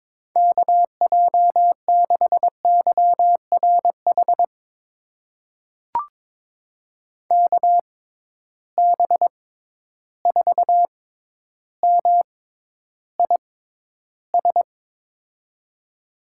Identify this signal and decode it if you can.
Morse code, heard as a keyed tone, about 22 wpm, spaced as for 11 wpm, 700 Hz. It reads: KJ6YRH KB4MIS
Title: USA Call Signs - 6 Characters - 6x Character Spacing - 22wpm